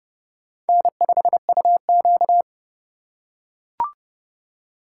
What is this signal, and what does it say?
Morse code, heard as a keyed tone, about 26 wpm, 700 Hz. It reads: N5UQ